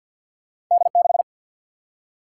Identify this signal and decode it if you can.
Morse code, heard as a keyed tone, about 50 wpm, 700 Hz. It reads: D 6
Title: Single-Letter Single-Number 50wpm